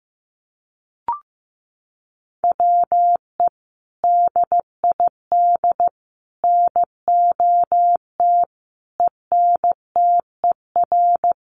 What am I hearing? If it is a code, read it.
Morse code, heard as a keyed tone, about 15 wpm, 700 Hz. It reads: WE DID NOT ENTER
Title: Sentences from Top 1300 Words 15wpm